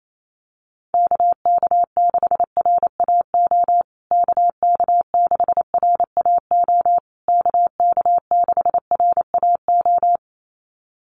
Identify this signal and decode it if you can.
Morse code, heard as a keyed tone, about 28 wpm, 700 Hz. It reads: KK6RAO KK6RAO KK6RAO
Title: USA Call Signs Repeated 3x 28wpm